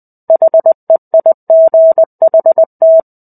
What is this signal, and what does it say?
Morse code, heard as a keyed tone, about 20 wpm, 650 Hz. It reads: HEIGHT
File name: 05-Chapter-05 20 WPM